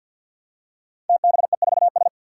Morse code, heard as a keyed tone, about 51 wpm, 700 Hz. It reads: TBE4S
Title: Five Challenging Letter-Number Combinations 50wpm